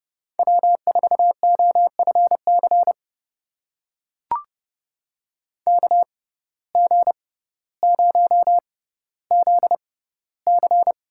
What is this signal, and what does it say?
Morse code, heard as a keyed tone, about 30 wpm, spaced as for 15 wpm, 700 Hz. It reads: W4OFC KG0ZC
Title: USA Call Signs - 5 Characters - 6x Character Spacing - 30wpm